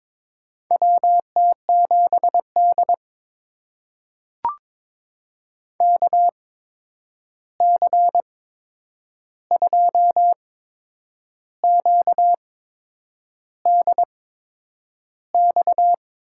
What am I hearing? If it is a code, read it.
Morse code, heard as a keyed tone, about 22 wpm, spaced as for 8 wpm, 700 Hz. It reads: WT7D KC2QDX
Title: USA Call Signs - 8x Character Spacing - 22wpm